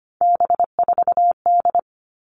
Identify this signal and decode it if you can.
Morse code, heard as a keyed tone, about 25 wpm, 700 Hz. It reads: B4D
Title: All Cut Numbers - 3-Digit 25wpm